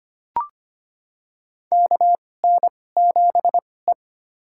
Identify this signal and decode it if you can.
Morse code, heard as a keyed tone, about 25 wpm, spaced as for 18 wpm, 700 Hz. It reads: KN7E
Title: USA Call Signs - 4 Characters - 2x Character Spacing - 25wpm